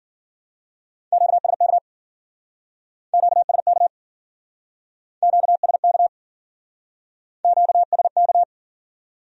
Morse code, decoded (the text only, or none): QSK QSK QSK QSK